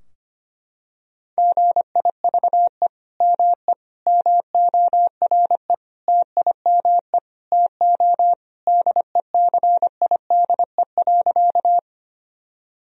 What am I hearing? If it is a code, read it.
Morse code, heard as a keyed tone, about 25 wpm, 700 Hz. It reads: GIVE ME MORE TIME TO DECIDE.
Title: Sentences from Top 1000 Words - Review Part 3 - 25wpm